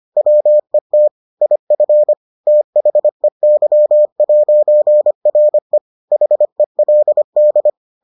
Morse code, heard as a keyed tone, about 25 wpm, 600 Hz. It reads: WET IF THEY'RE HELD